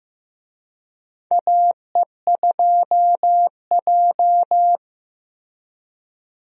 Morse code, heard as a keyed tone, about 15 wpm, 700 Hz. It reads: AE2J
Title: USA Call Signs - 4 Characters - Repeated 3x - 15wpm